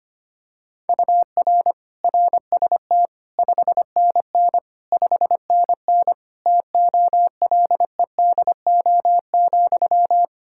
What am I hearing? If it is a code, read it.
Morse code, heard as a keyed tone, about 25 wpm, 700 Hz. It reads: UR RST 5NN 5NN TOLEDO,
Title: Practice QSO - Exchange 1 - Beginning - 25wpm